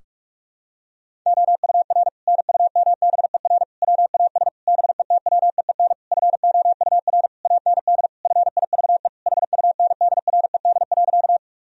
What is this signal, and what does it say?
Morse code, heard as a keyed tone, about 41 wpm, 700 Hz. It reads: OUR NUMBER WAS BETWEEN FOUR AND FIVE HUNDRED.
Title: Sentences from Top 1000 Words - Review Part 2 - 45wpm